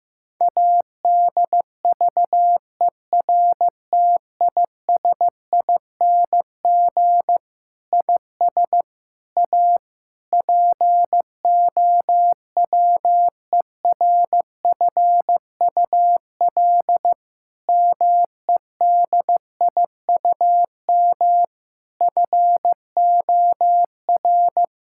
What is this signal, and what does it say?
Morse code, heard as a keyed tone, about 15 wpm, 700 Hz. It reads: ADVERTISING IS A POWERFUL MEDIUM FOR
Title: Sentences from Top 1000 Words - Review Part 6 - 15wpm